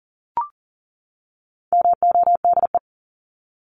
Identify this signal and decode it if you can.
Morse code, heard as a keyed tone, about 39 wpm, 700 Hz. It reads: MODE
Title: Top 1751-2000 Words 40wpm